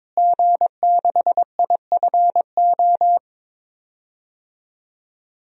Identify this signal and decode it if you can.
Morse code, heard as a keyed tone, about 22 wpm, 700 Hz. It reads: G6IFO
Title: DXCC-100 Call Signs - 5 Characters - Repeated 3x - 22wpm